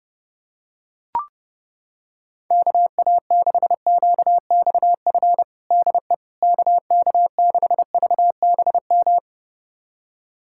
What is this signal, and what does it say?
Morse code, heard as a keyed tone, about 30 wpm, 700 Hz. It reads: KA6QXF DE KK6VBM
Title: USA Callsign DE Callsign - 6 Characters - 30wpm